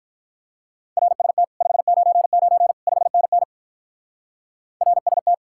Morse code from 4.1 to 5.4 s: RST

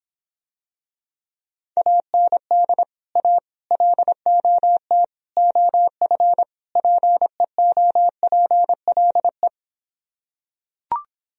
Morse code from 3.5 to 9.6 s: LOT OF PEOPLE

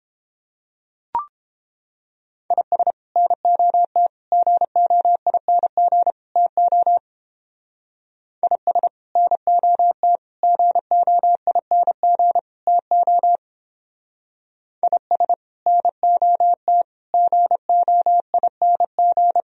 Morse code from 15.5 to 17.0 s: NOT

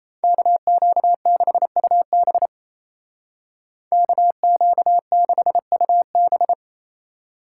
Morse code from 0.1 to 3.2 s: KQ6UB

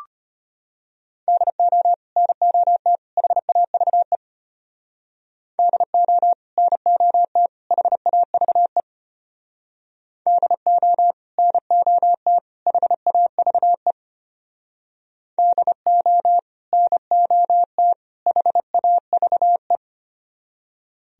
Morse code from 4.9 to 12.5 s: DO NOT HAVE DO NOT

